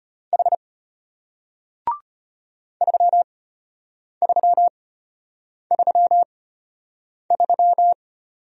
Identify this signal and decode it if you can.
Morse code, heard as a keyed tone, about 33 wpm, spaced as for 8 wpm, 700 Hz. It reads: H 3333